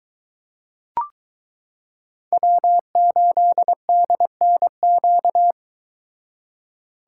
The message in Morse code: W8DNQ

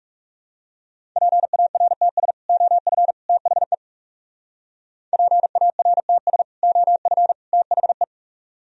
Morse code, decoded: PARTS OF THE PARTS OF THE

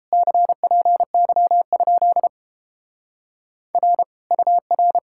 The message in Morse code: CPY? R UR

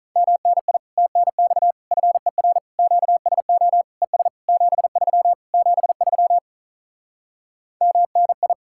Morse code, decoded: MNI TNX FER QSO ES 73 73 MNI

